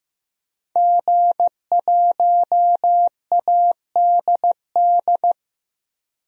Morse code, heard as a keyed tone, about 15 wpm, 700 Hz. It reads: G1ADD
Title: DXCC-100 Call Signs - 5 Characters - 15wpm